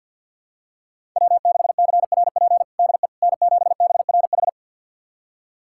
Morse code, heard as a keyed tone, about 48 wpm, 700 Hz. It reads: W6CRP DE N7BRH
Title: USA Callsign DE Callsign - 5 Characters - 50wpm